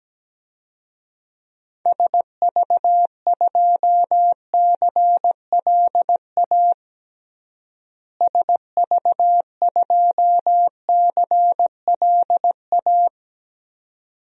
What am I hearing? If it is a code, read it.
Morse code, heard as a keyed tone, about 17 wpm, 700 Hz. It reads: SV2CLA SV2CLA
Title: DXCC-100 Call Signs - 6 Characters - Repeated 2x - 17wpm